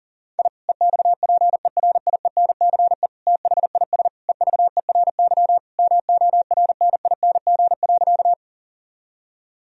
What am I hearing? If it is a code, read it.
Morse code, heard as a keyed tone, about 37 wpm, 700 Hz. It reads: I EXPERIENCE THIS EVERY MORNING.
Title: Sentences from Top 1000 Words - Review Part 3 - 40wpm